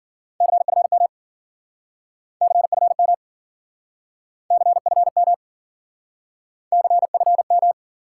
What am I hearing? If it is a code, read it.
Morse code, heard as a keyed tone, about 52 wpm, 700 Hz. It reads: CFM CFM CFM CFM